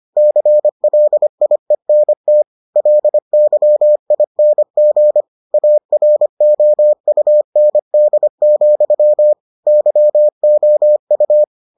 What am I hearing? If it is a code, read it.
Morse code, heard as a keyed tone, about 25 wpm, 600 Hz. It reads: CLIENT LYING AROUND, YOU